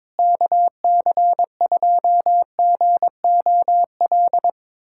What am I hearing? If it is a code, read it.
Morse code, heard as a keyed tone, about 21 wpm, 700 Hz. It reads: KC2GOL
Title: USA Callsign DE Callsign 22wpm